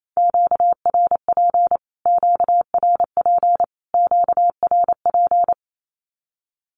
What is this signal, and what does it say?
Morse code, heard as a keyed tone, about 28 wpm, 700 Hz. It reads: QRP QRP QRP
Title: Q-Codes Repeated 3x 28wpm